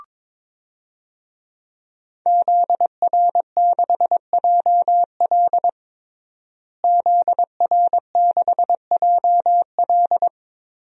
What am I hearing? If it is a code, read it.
Morse code, heard as a keyed tone, about 22 wpm, 700 Hz. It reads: ZR6JL ZR6JL
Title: DXCC-100 Call Signs - 5 Characters - Repeated 2x - 22wpm